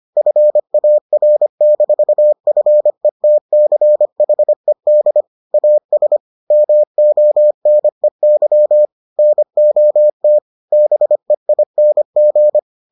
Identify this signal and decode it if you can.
Morse code, heard as a keyed tone, about 25 wpm, 600 Hz. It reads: FAR-FETCHED AS MONEY NOT BEING